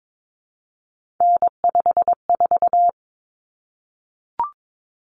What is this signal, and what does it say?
Morse code, heard as a keyed tone, about 22 wpm, 700 Hz. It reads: N54
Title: Common Cut Numbers - 3-Digit 22wpm